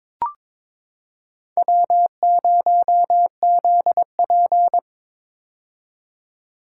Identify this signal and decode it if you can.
Morse code, heard as a keyed tone, about 22 wpm, 700 Hz. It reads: W0ZP
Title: K1USN SST QSO Practice - 22wpm